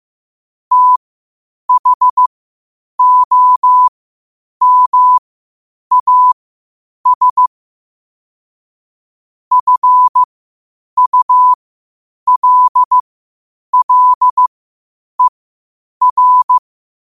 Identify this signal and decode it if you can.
Morse code, heard as a keyed tone, about 15 wpm, spaced as for 8 wpm, 1000 Hz. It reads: THOMAS FULLER
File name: qotdfiles / 8wpm